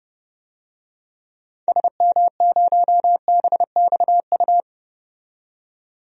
Morse code, heard as a keyed tone, about 32 wpm, 700 Hz. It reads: SM0BXU
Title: DXCC-100 Call Signs - 6 Characters - 30wpm